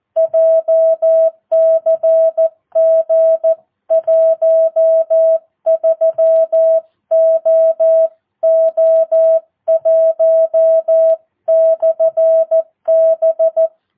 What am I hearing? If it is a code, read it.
Morse code, heard as a keyed tone, about 14 wpm, 660 Hz. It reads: JCG 13OO1/B
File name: jh8cbh / 2022cw